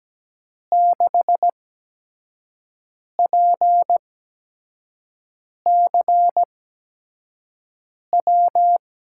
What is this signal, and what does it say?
Morse code, heard as a keyed tone, about 17 wpm, spaced as for 5 wpm, 700 Hz. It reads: 6PCW
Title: USA Call Signs - 5 Characters - 8x Character Spacing - 17wpm